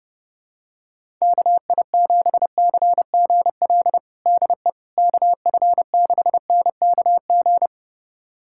Morse code, heard as a keyed tone, about 32 wpm, 700 Hz. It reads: KI7CGL DE KF6NKG
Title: USA Callsign DE Callsign - 6 Characters - 30wpm